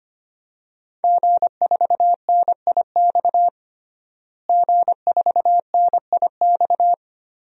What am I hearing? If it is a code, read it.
Morse code, heard as a keyed tone, about 25 wpm, 700 Hz. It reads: G4NIX G4NIX
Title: DXCC-100 Call Signs - 5 Characters - Repeated 3x - 25wpm